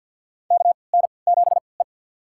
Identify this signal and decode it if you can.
Morse code, heard as a keyed tone, about 47 wpm, spaced as for 28 wpm, 700 Hz. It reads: KN7E
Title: USA Call Signs - 3x Character Spacing - 50wpm